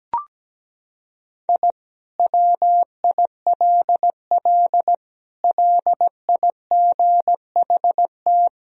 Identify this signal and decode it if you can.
Morse code, heard as a keyed tone, about 17 wpm, 700 Hz. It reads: I WILL LIGHT
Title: Sentences from Top 1000 Words - Review Part 1 - 17wpm